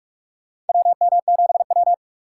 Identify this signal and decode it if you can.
Morse code, heard as a keyed tone, about 44 wpm, 700 Hz. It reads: WM7W
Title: USA Call Signs - 4 Characters - 4x Character Spacing - 45wpm